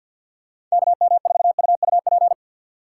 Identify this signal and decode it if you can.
Morse code, heard as a keyed tone, about 52 wpm, 700 Hz. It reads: KM4UUP
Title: USA Call Signs - 2x Character Spacing - 50wpm